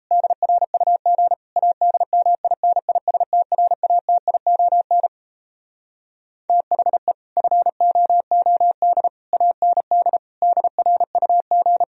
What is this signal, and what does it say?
Morse code, heard as a keyed tone, about 37 wpm, 700 Hz. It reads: DRUG ADMINISTRATION THE FOOD AND DRUG